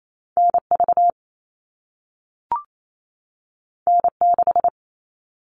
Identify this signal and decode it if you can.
Morse code, heard as a keyed tone, about 28 wpm, 700 Hz. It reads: NV N6
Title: All Cut Numbers - 2-Digit 28wpm